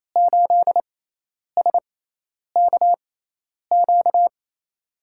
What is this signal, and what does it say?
Morse code, heard as a keyed tone, about 28 wpm, spaced as for 10 wpm, 700 Hz. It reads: 8SKQ
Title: USA Call Signs - 6x Character Spacing - 28wpm